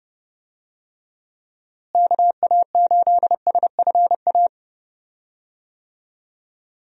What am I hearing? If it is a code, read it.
Morse code, heard as a keyed tone, about 32 wpm, 700 Hz. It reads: KA8SFA
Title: USA Call Signs - 6 Characters - Repeated 2x - 30wpm